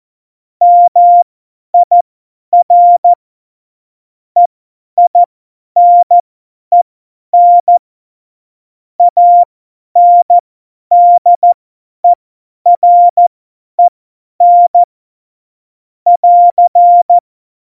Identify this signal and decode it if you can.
Morse code, heard as a keyed tone, about 14 wpm, spaced as for 10 wpm, 700 Hz. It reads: MIR EINEN ANDEREN <AR>
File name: Morsekurs / Witze / Witz 23. Juli 2019, 50 F70